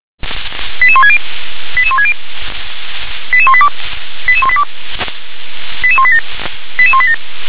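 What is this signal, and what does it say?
Five-tone selective call (ZVEI), heard as quick runs of five tones, tones of about 70 ms. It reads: 88160 88160 88182 88182 88187 88187
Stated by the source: ZVEI five-tone selective call 88148, received on SDR.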